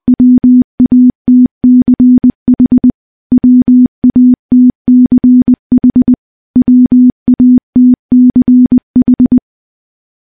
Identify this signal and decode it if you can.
Morse code, heard as a keyed tone, about 20 wpm, 250 Hz. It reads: WATCH WATCH WATCH
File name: morse-training-20WPM